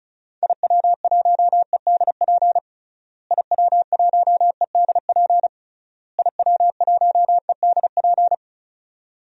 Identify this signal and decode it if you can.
Morse code, heard as a keyed tone, about 34 wpm, 700 Hz. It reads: IW1EDP IW1EDP IW1EDP
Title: DXCC-100 Call Signs - 6 Characters - Repeated 3x - 35wpm